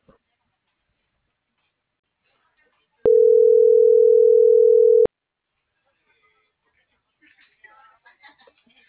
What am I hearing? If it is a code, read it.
Ringback tone (North America), 440 + 480 Hz.